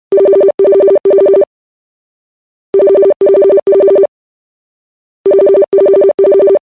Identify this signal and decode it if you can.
There is no code.